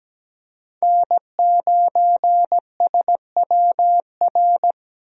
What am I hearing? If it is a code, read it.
Morse code, heard as a keyed tone, about 17 wpm, 700 Hz. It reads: N9SWR
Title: USA Call Signs - 5 Characters - 17wpm